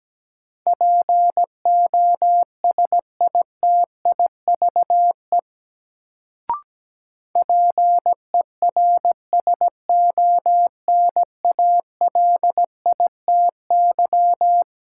Morse code, heard as a keyed tone, about 17 wpm, 700 Hz. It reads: POSITIVE PERSONALITY
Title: Sentences from Top 1400 Words - New Word Drill Alt - 17wpm